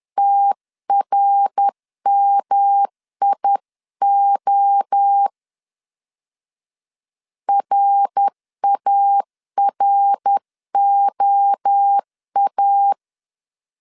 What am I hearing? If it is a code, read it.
Morse code, heard as a keyed tone, about 10 wpm, 800 Hz. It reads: TRMIO RAROA
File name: 10wpm